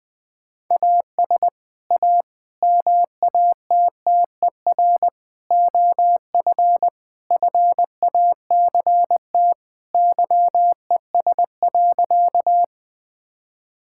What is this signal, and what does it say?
Morse code, heard as a keyed tone, about 20 wpm, 700 Hz. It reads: AS A MATTER OF FACT YES.
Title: Sentences with 4-Word Phrases - Spoken Beginning - 20wpm